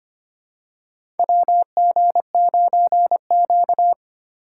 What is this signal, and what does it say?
Morse code, heard as a keyed tone, about 25 wpm, 690 Hz. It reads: WG9Q
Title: USA Call Signs - 4 Characters - 4x Character Spacing - 25wpm